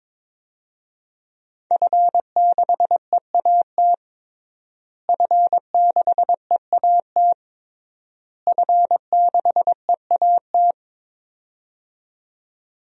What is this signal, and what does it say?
Morse code, heard as a keyed tone, about 22 wpm, 700 Hz. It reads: F6EAT F6EAT F6EAT
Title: DXCC-100 Call Signs - 5 Characters - Repeated 3x - 22wpm